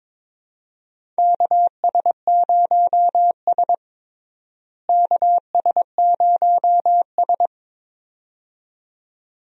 Morse code, heard as a keyed tone, about 22 wpm, 700 Hz. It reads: KS0S KS0S
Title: USA Call Signs - 4 Characters - Repeated 2x - 22wpm